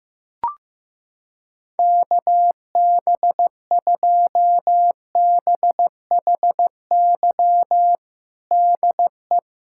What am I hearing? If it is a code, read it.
Morse code, heard as a keyed tone, about 15 wpm, 700 Hz. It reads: KB2BHY DE
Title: USA Callsign DE Callsign - 6 Characters - 15wpm